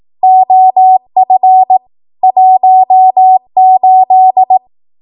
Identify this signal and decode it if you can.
Morse code, heard as a keyed tone, about 18 wpm, 750 Hz. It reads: OF 18
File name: Archive / 18 WPM / 18WPM